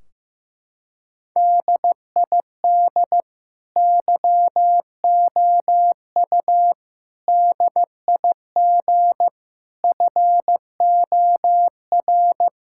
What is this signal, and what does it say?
Morse code, heard as a keyed tone, about 15 wpm, 700 Hz. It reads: DID YOU DIG FOR
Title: Sentences from Top 1600 Words 15wpm